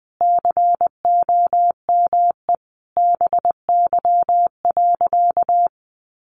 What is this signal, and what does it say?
Morse code, heard as a keyed tone, about 20 wpm, 700 Hz. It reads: COME BY.
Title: Sentences from Top 1500 Words 20wpm